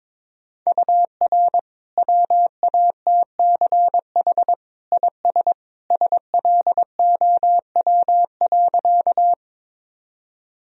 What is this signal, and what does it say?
Morse code, heard as a keyed tone, about 21 wpm, 700 Hz. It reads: UR WATCH IS SLOW.
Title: Sentences from Top 400 Words and Abbreviations 22wpm